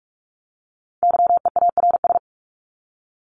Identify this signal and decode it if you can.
Morse code, heard as a keyed tone, about 44 wpm, 700 Hz. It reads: YEARS
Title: ES:S Suffix - Top 100 Words with Base Words 45wpm